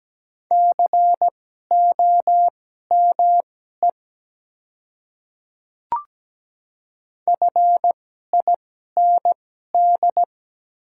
Morse code, heard as a keyed tone, about 17 wpm, spaced as for 12 wpm, 700 Hz. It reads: COME FIND
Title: 2x Character Spacing - Top 300 Words - 17wpm